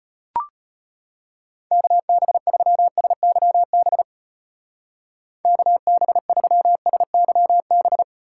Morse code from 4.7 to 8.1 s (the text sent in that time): KB3SYB